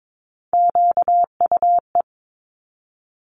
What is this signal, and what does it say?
Morse code, heard as a keyed tone, about 22 wpm, 700 Hz. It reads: QUE
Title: Multi-letter Phonemes 22wpm